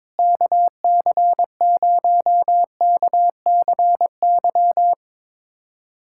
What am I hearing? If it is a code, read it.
Morse code, heard as a keyed tone, about 21 wpm, 700 Hz. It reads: KC0KCY
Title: USA Callsign DE Callsign 22wpm